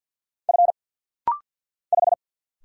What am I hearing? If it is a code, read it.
Morse code, heard as a keyed tone, about 50 wpm, 700 Hz. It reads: F 5
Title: Single Letter-Number - Rapid-Fire 50wpm